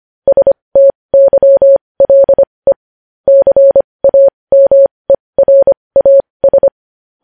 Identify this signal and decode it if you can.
Morse code, heard as a keyed tone, about 25 wpm, spaced as for 20 wpm, 550 Hz. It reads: STYLE CAMERAS
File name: podcast / News headlines-1x-25x20